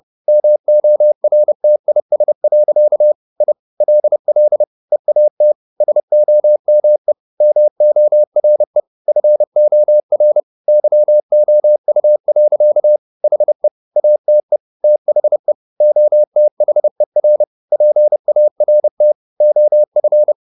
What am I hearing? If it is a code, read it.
Morse code, heard as a keyed tone, about 30 wpm, 600 Hz. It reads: MORTIS. I LL EAT SOME MORE FOR YOU. HE ATE THE OTHER PART OF
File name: the old man and the sea / the old man and the sea 1 30wpm